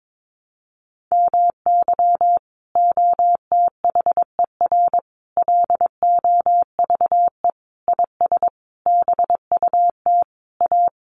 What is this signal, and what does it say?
Morse code, heard as a keyed tone, about 22 wpm, 700 Hz. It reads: MY OTHER LOVE IS BUT A
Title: Sentences from Top 1300 Words 22wpm